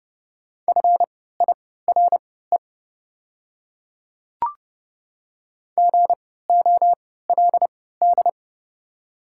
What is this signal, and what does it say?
Morse code, heard as a keyed tone, about 29 wpm, spaced as for 17 wpm, 700 Hz. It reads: FIRE GOLD
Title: 3x Character Spacing - Top 500 Words - 30wpm